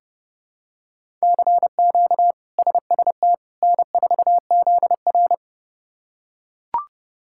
Morse code, heard as a keyed tone, about 30 wpm, 700 Hz. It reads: CQ SST N4ZR
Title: K1USN SST QSO Practice - 30wpm